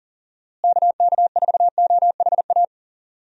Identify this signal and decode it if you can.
Morse code, heard as a keyed tone, about 40 wpm, 700 Hz. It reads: KK4OHA
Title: USA Call Signs - 6 Characters - Repeated 3x - 40wpm